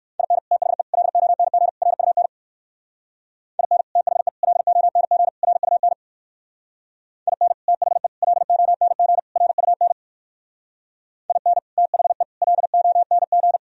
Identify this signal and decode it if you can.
Morse code, heard as a keyed tone, about 59 wpm, 700 Hz. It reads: IN THE LONG RUN IN THE LONG RUN IN THE LONG RUN IN THE LONG